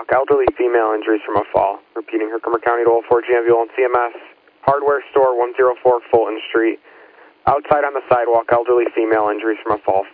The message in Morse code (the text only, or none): none